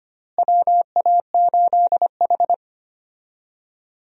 Morse code, heard as a keyed tone, about 25 wpm, 700 Hz. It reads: WA8H
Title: USA Call Signs - 4 Characters - Repeated 3x - 25wpm